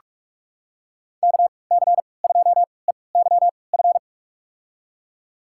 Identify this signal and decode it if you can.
Morse code, heard as a keyed tone, about 46 wpm, spaced as for 25 wpm, 700 Hz. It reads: KC2EYF
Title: USA Call Signs - 6 Characters - 3x Character Spacing - 45wpm